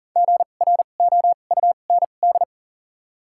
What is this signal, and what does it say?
Morse code, heard as a keyed tone, about 40 wpm, spaced as for 29 wpm, 700 Hz. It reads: GROUND
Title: 2x Character Spacing - Top 500 Words - 40wpm